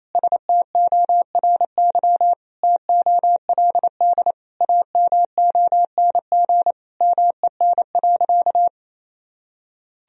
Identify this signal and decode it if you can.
Morse code, heard as a keyed tone, about 28 wpm, 700 Hz. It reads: STORY TOLD AMONG MEN.
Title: Sentences from Top 1000 Words - Review Part 3 - 28wpm